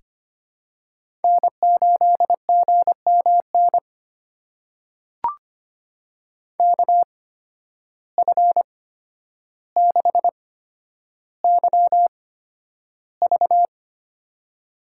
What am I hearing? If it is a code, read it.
Morse code, heard as a keyed tone, about 25 wpm, spaced as for 11 wpm, 700 Hz. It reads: N8GMN KF6YV
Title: USA Call Signs - 5 Characters - 8x Character Spacing - 25wpm